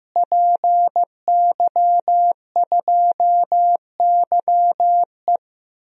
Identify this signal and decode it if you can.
Morse code, heard as a keyed tone, about 15 wpm, 700 Hz. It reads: PY2YE
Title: DXCC-100 Call Signs - 5 Characters - Repeated 2x - 15wpm